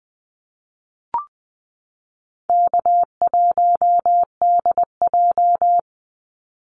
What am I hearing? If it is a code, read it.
Morse code, heard as a keyed tone, about 20 wpm, 700 Hz. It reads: K1DJ
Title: Easy Call Signs Repeated 2x - 20wpm